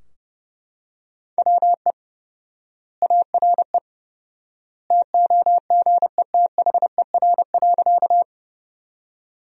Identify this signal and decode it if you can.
Morse code, heard as a keyed tone, about 31 wpm, 700 Hz. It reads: WE ARE TOGETHER.
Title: 4x Word-Spacing - Sentences from Top 300 Words - 30wpm